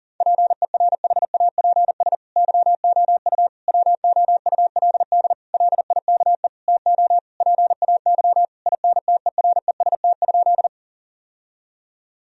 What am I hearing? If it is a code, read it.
Morse code, heard as a keyed tone, about 40 wpm, 700 Hz. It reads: PERHAPS YOU WOULD LIKE TO PAY INTEREST?
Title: Sentences from Top 1000 Words - Review Part 2 - 40wpm